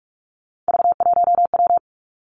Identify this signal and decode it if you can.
Morse code, heard as a keyed tone, about 46 wpm, 700 Hz. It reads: V1W
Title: Three Challenging Letter-Number Combinations 45wpm